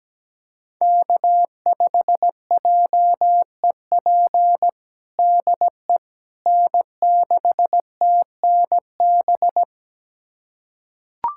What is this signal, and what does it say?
Morse code, heard as a keyed tone, about 17 wpm, 700 Hz. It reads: K5JEP DE N6TNB
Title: USA Callsign DE Callsign - 5 Characters - 17wpm